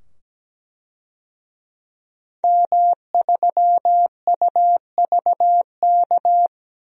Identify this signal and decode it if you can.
Morse code, heard as a keyed tone, about 17 wpm, 700 Hz. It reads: M3UVK
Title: DXCC-100 Call Signs - 5 Characters - 17wpm